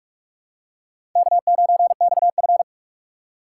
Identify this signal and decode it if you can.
Morse code, heard as a keyed tone, about 47 wpm, 700 Hz. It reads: K9XF
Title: USA Call Signs - 6x Character Spacing - 45wpm